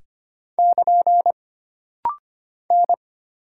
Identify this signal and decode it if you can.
Morse code, heard as a keyed tone, about 25 wpm, 700 Hz. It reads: <KN> N
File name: Single Character (Letter:Number:Punctuation:Prosign) - Rapid-Fire - 25wpm